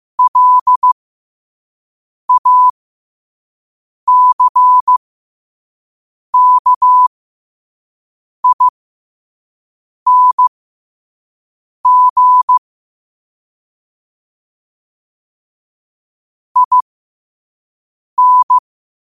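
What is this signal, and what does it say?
Morse code, heard as a keyed tone, about 15 wpm, spaced as for 5 wpm, 1000 Hz. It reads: LACKING IN
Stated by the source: Morse code practice at 5 wpm.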